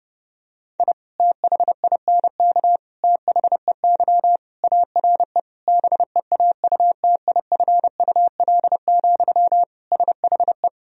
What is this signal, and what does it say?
Morse code, heard as a keyed tone, about 32 wpm, 700 Hz. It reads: I THINK THEY ARE BEAUTIFUL, SHE
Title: Sentences from Top 1000 Words - Review Part 4 - 30wpm